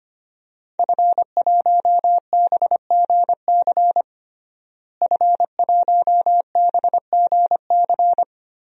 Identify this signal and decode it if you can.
Morse code, heard as a keyed tone, about 25 wpm, 700 Hz. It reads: F1BGC F1BGC
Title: DXCC-100 Call Signs - 5 Characters - Repeated 3x - 25wpm